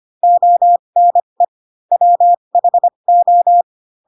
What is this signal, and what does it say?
Morse code, heard as a keyed tone, about 25 wpm, spaced as for 22 wpm, 700 Hz. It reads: ONE WHO
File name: Pinocchio / Pinocchio ch2 18 25 ik1hge